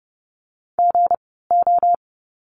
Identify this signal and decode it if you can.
Morse code, heard as a keyed tone, about 30 wpm, spaced as for 17 wpm, 700 Hz. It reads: GO
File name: course / 30:10wpm / 230 Lesson 37f - Letter x - Words 30wpm